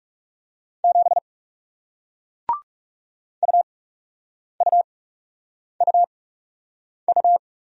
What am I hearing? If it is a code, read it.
Morse code, heard as a keyed tone, about 40 wpm, spaced as for 8 wpm, 700 Hz. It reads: 7 UUUU